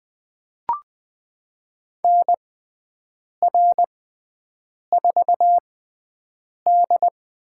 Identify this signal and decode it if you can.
Morse code, heard as a keyed tone, about 20 wpm, spaced as for 7 wpm, 700 Hz. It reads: NR4D
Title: USA Call Signs - 4 Characters - 6x Character Spacing - 20wpm